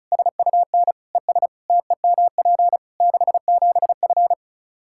Morse code, heard as a keyed tone, about 34 wpm, 700 Hz. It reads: SUN ES TEMP 67F
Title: Practice QSO - Exchange 2 - Info - 35wpm